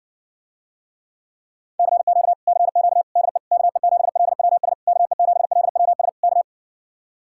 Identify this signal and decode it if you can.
Morse code, heard as a keyed tone, about 59 wpm, 700 Hz. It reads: CQ CQ DE KE7LWS KE7LWS K